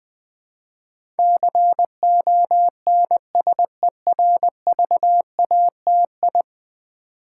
Morse code, heard as a keyed tone, about 20 wpm, 700 Hz. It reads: CONSERVATI
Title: Build A Word - 5 Letters At A Time - 12-Letter Words - 20wpm